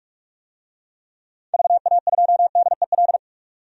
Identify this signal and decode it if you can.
Morse code, heard as a keyed tone, about 46 wpm, 700 Hz. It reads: VA2DEL
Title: DXCC-100 Call Signs - 6 Characters - 45wpm